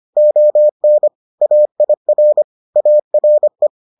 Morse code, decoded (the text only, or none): ON AIR ARE